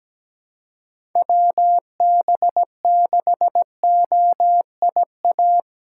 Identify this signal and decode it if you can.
Morse code, heard as a keyed tone, about 17 wpm, 700 Hz. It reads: WB6OIA
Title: USA Call Signs - 6 Characters - 4x Character Spacing - 17wpm